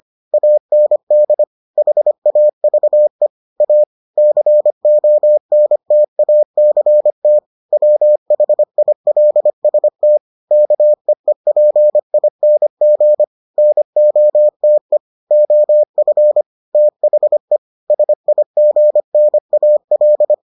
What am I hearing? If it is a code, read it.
Morse code, heard as a keyed tone, about 25 wpm, 600 Hz. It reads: AND HAVE A CONTACT WHILST KEEPING NOTE OF THE SIGNAL